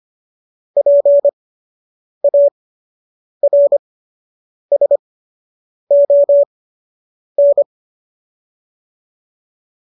Morse code, heard as a keyed tone, about 25 wpm, spaced as for 8 wpm, 570 Hz. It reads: PARSON
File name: qso003 08